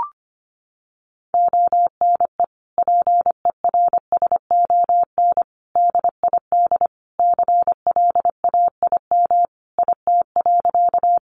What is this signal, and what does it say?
Morse code, heard as a keyed tone, about 25 wpm, 700 Hz. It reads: ONE PERSON DID CLAIM IT.
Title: Sentences from Top 1500 Words 25wpm